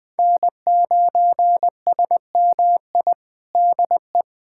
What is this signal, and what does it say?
Morse code, heard as a keyed tone, about 20 wpm, 700 Hz. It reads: N9SMI DE
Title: USA Callsign DE Callsign - 5 Characters - 20wpm